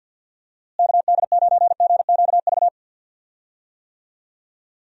Morse code, heard as a keyed tone, about 46 wpm, 700 Hz. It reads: KD9GQV
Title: USA Call Signs - 6 Characters - Repeated 3x - 50wpm